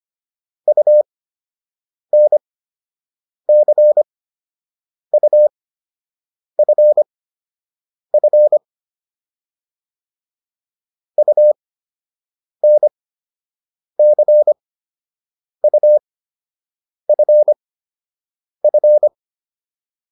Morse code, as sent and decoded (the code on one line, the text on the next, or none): ..- -. -.-. ..- ..-. ..-. / ..- -. -.-. ..- ..-. ..-.
UNCUFF UNCUFF